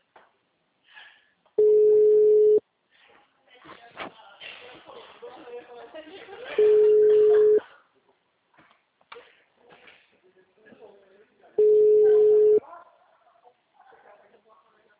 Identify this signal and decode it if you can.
Ringback tone (Europe, 425 Hz), with speech.
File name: out- -2004- - - .53686